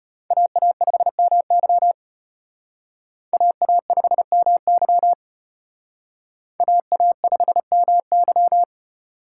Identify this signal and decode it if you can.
Morse code, heard as a keyed tone, about 31 wpm, 700 Hz. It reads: AA5MY AA5MY AA5MY
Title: Speed Racing - USA 2x2 Call Signs 25wpm